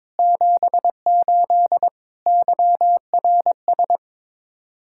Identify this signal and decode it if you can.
Morse code, heard as a keyed tone, about 22 wpm, 700 Hz. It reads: 78 YRS
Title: Speed-Racing Common QSO Phrases - 22wpm